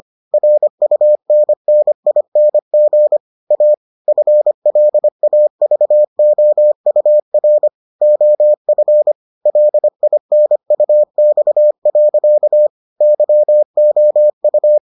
Morse code, heard as a keyed tone, about 25 wpm, 600 Hz. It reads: RUNNING A FLAVOUR OF LINUX. YOU